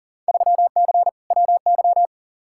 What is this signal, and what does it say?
Morse code, heard as a keyed tone, about 40 wpm, 700 Hz. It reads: 3C WY
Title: Field Day Exchanges Encore - 40wpm